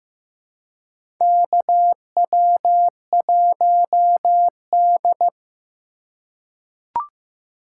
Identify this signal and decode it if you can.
Morse code, heard as a keyed tone, about 15 wpm, 700 Hz. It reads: KW1D
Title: USA Call Signs - 4 Characters - 4x Character Spacing - 15wpm